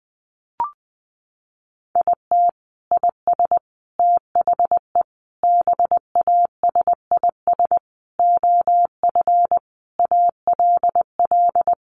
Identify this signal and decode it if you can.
Morse code, heard as a keyed tone, about 19 wpm, 700 Hz. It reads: IT IS THE BASIS OF ALL
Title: Sentences from Top 1200 Words 20wpm